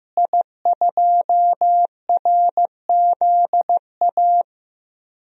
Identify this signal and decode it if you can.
Morse code, heard as a keyed tone, about 15 wpm, 700 Hz. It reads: I2RZA
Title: DXCC-100 Call Signs - 5 Characters - Repeated 2x - 15wpm